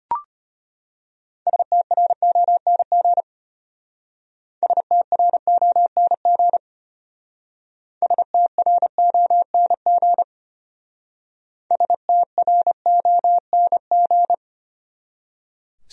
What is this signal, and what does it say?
Morse code, heard as a keyed tone, about 32 wpm, 700 Hz. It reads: STRONG STRONG STRONG STRONG